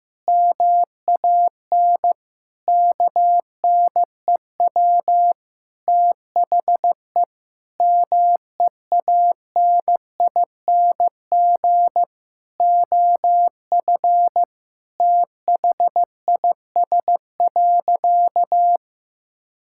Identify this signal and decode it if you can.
Morse code, heard as a keyed tone, about 16 wpm, 700 Hz. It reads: MAN KNEW THE MEANING OF THIS.
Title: Sentences from Top 1000 Words - Review Part 6 - 15wpm